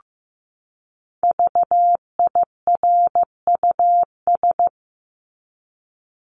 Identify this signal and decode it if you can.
Morse code, heard as a keyed tone, about 15 wpm, 700 Hz. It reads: VIRUS
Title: Top 1251-1500 Words 15wpm